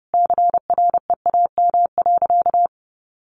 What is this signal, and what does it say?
Morse code, heard as a keyed tone, about 30 wpm, 700 Hz. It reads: CREAM.
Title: Sentences from Top 1500 Words 30wpm